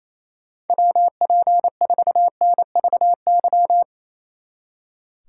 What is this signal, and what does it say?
Morse code, heard as a keyed tone, about 35 wpm, 700 Hz. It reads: WP4NVY